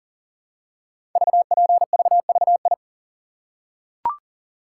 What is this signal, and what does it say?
Morse code, heard as a keyed tone, about 39 wpm, 700 Hz. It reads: VPVVI
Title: Code Groups - 5 Letters - 40wpm